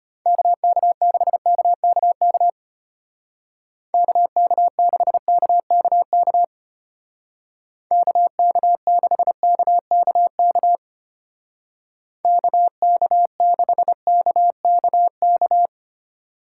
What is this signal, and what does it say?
Morse code, heard as a keyed tone, about 33 wpm, 700 Hz. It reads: KK6KKK KK6KKK KK6KKK KK6KKK